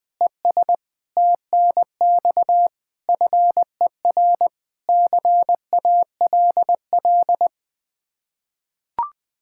Morse code, heard as a keyed tone, about 21 wpm, 700 Hz. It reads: ES TNX FER CALL
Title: Practice QSO - Exchange 1 - Beginning - 20wpm